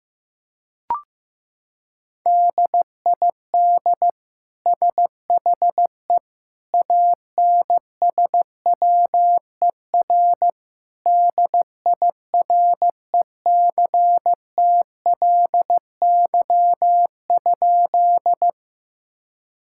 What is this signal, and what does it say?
Morse code, heard as a keyed tone, about 15 wpm, 700 Hz. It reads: DID SHE ANSWER DIRECTLY?
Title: Sentences from Top 1000 Words - Review Part 5 - 15wpm